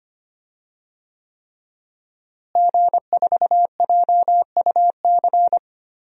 Morse code, heard as a keyed tone, about 25 wpm, 700 Hz. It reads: G4JUC
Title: DXCC-100 Call Signs - 5 Characters - Repeated 3x - 25wpm